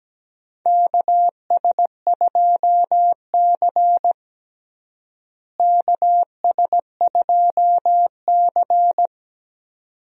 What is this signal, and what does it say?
Morse code, heard as a keyed tone, about 17 wpm, 700 Hz. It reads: KS2C KS2C
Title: USA Call Signs - 4 Characters - Repeated 2x - 17wpm